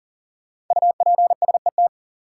Morse code, heard as a keyed tone, about 39 wpm, 700 Hz. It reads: UPSET
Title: Sentences from Top 800 Words - New Word Drill - 40wpm